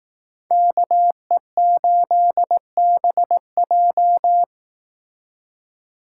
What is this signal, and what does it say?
Morse code, heard as a keyed tone, about 18 wpm, 700 Hz. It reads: KE8BJ
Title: Speed Racing - USA 2x2 Call Signs 15wpm